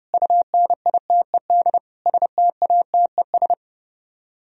Morse code, heard as a keyed tone, about 33 wpm, 700 Hz. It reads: UNITED STATES